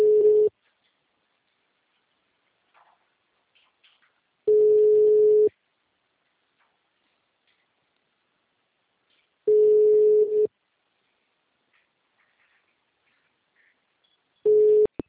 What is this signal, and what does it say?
Ringback tone (Europe, 425 Hz).
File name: out- -2006- - - .273